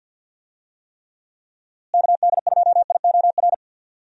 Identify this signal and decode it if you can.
Morse code, heard as a keyed tone, about 52 wpm, 700 Hz. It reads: KD2IOR
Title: USA Call Signs - 6 Characters - Repeated 2x - 50wpm